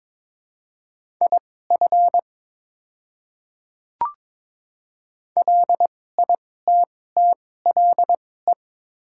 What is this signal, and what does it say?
Morse code, heard as a keyed tone, about 21 wpm, spaced as for 16 wpm, 700 Hz. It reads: IF LITTLE
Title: 2x Character Spacing - Top 200 Words - 22wpm